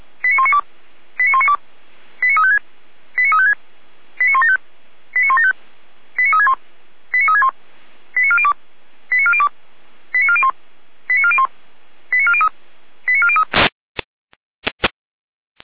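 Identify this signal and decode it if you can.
Five-tone selective call (ZVEI), heard as quick runs of five tones, tones of about 80 ms. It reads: 89192 89192 89356 89356 89175 89175 89361 89361 89402 89402 89401 89401 89403 89403
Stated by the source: ZVEI five-tone call 89560 received by SDR.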